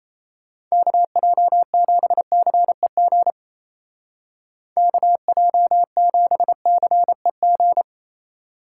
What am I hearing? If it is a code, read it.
Morse code, heard as a keyed tone, about 34 wpm, 700 Hz. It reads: KJ7CEG KJ7CEG
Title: Speed Racing - USA Call Signs 28wpm